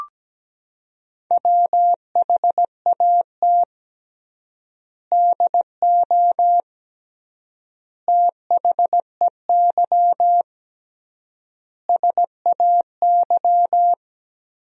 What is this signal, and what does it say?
Morse code, heard as a keyed tone, about 17 wpm, 700 Hz. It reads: WHAT DO THEY SAY
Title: 3x Word-Spacing - Sentences from Top 200 Words - 17wpm